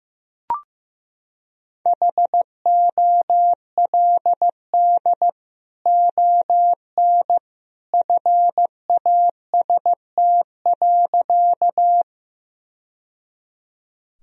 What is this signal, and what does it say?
Morse code, heard as a keyed tone, about 15 wpm, 700 Hz. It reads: HOLD ON FAST.
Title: Sentences from Top 1000 Words - Review Part 2 - 15wpm